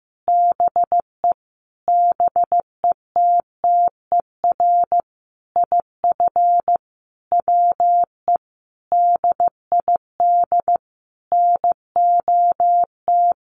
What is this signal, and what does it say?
Morse code, heard as a keyed tone, about 15 wpm, 700 Hz. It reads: BE BETTER IF WE DID NOT
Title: Sentences from Top 1100 Words 15wpm